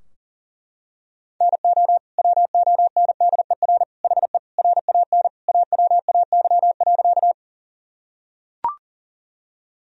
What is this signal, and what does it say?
Morse code, heard as a keyed tone, about 40 wpm, 700 Hz. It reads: NO WONDER HE RAN AWAY.
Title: Sentences from Top 1000 Words - Review Part 3 - 40wpm